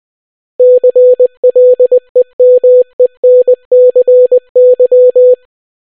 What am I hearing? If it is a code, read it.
Morse code, heard as a keyed tone, about 20 wpm, 500 Hz. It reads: CLEMENCY